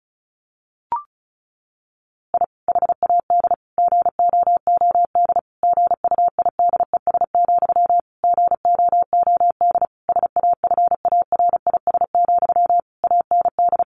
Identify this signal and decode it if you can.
Morse code, heard as a keyed tone, about 34 wpm, 700 Hz. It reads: I HAD GOOD GUIDES, GOOD SAFARIS, AND
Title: A Sound Of Thunder 35wpm